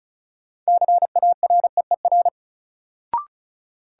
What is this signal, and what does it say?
Morse code, heard as a keyed tone, about 34 wpm, 700 Hz. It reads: CAREER
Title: Sentences from Top 1000 Words - New Word Drill - 35wpm